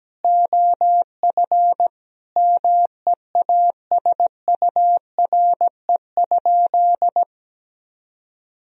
Morse code, heard as a keyed tone, about 17 wpm, 700 Hz. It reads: OF MEASURE?
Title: Sentences from Top 1000 Words - Review Part 2 - 17wpm